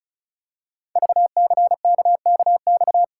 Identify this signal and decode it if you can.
Morse code, heard as a keyed tone, about 36 wpm, 700 Hz. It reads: VCKKX
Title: Code Groups - 5 Letters - 35wpm